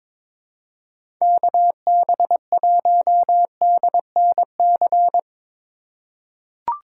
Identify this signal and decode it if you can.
Morse code, heard as a keyed tone, about 22 wpm, 700 Hz. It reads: KB1DNC
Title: USA Call Signs - 6 Characters - 2x Character Spacing - 22wpm